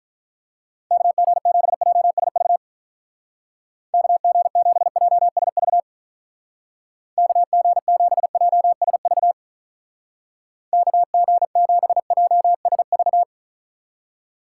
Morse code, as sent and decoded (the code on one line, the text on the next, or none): -.- --. --... .--- ... ...- / -.- --. --... .--- ... ...- / -.- --. --... .--- ... ...- / -.- --. --... .--- ... ...-
KG7JSV KG7JSV KG7JSV KG7JSV